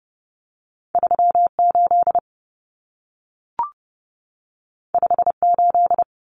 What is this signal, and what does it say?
Morse code, heard as a keyed tone, about 30 wpm, 700 Hz. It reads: 38 58
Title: Common Cut Numbers - 2-Digit 30wpm